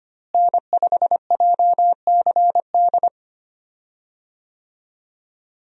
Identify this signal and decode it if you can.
Morse code, heard as a keyed tone, about 25 wpm, 700 Hz. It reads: N5JCD
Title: USA Call Signs - 5 Characters - Repeated 2x - 25wpm